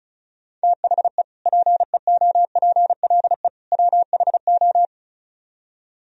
Morse code, heard as a keyed tone, about 45 wpm, 700 Hz. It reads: THE PEOPLE WHO